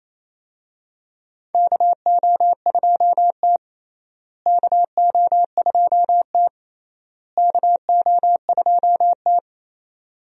Morse code, heard as a keyed tone, about 28 wpm, 700 Hz. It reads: KO2T KO2T KO2T
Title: USA Call Signs - 4 Characters - Repeated 3x - 28wpm